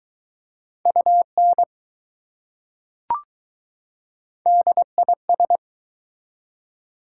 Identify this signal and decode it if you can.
Morse code, heard as a keyed tone, about 20 wpm, 700 Hz. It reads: UN DIS